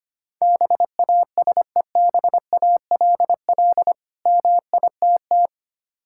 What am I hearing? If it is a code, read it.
Morse code, heard as a keyed tone, about 25 wpm, 690 Hz. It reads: BASEBALL MITT
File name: practice / 25wpm / Difficult Words with Hints - 25wpm